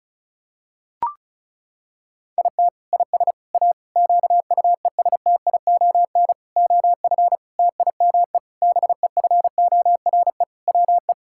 Morse code, heard as a keyed tone, about 36 wpm, 700 Hz. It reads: IT IS A QUESTION OF TIME BEFORE WE
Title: Sentences from Top 1000 Words - Review Part 6 - 35wpm